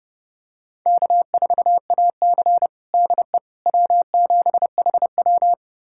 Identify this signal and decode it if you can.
Morse code, heard as a keyed tone, about 32 wpm, 700 Hz. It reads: K4AC DE W7HW
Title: USA Callsign DE Callsign - 4 Characters - 30wpm